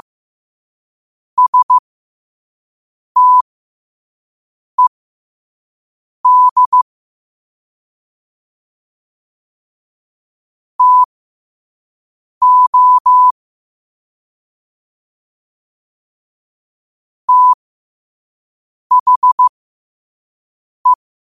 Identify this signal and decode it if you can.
Morse code, heard as a keyed tone, about 15 wpm, spaced as for 5 wpm, 1000 Hz. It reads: STED TO THE
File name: qotdfiles / 5wpm